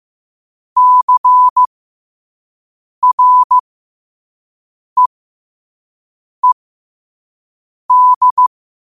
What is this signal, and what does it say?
Morse code, heard as a keyed tone, about 15 wpm, spaced as for 5 wpm, 1000 Hz. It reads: CREED
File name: qotdfiles / 5wpm